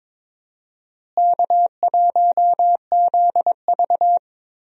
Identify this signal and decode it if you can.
Morse code, heard as a keyed tone, about 22 wpm, 700 Hz. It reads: K1ZV
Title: USA Call Signs - 4 Characters - Repeated 2x - 22wpm